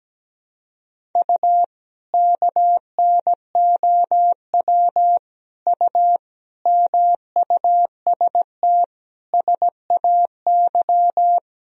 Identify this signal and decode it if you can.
Morse code, heard as a keyed tone, about 17 wpm, 700 Hz. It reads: U KNOW U MUST SAY
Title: Sentences from Top 500 Words and Abbreviations 17wpm